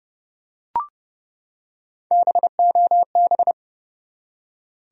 Morse code, heard as a keyed tone, about 30 wpm, 700 Hz. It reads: BOB
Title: USA Common Names - Max 4 Letters - 30wpm